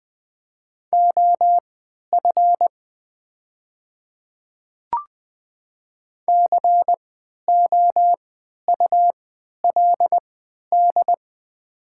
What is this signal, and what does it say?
Morse code, heard as a keyed tone, about 20 wpm, spaced as for 11 wpm, 700 Hz. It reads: OF COULD
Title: 3x Character Spacing - Top 100 Words - 20wpm